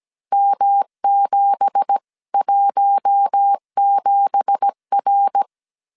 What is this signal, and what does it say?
Morse code, heard as a keyed tone, about 17 wpm, 800 Hz. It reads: M717R
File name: CW / DET / 1 9 1500C S A 16wpm 2828